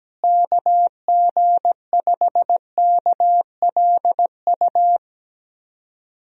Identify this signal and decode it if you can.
Morse code, heard as a keyed tone, about 17 wpm, 700 Hz. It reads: KG5KLU
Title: USA Call Signs - 6 Characters - Repeated 2x - 17wpm